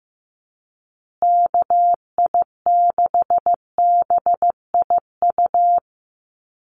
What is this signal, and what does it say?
Morse code, heard as a keyed tone, about 15 wpm, 700 Hz. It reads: KI6BIU
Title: USA Call Signs Repeated 3x 15wpm